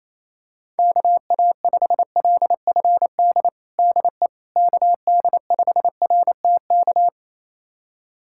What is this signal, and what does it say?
Morse code, heard as a keyed tone, about 28 wpm, 700 Hz. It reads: KA5LFD DE KD5RTK
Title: USA Callsign DE Callsign - 6 Characters - 28wpm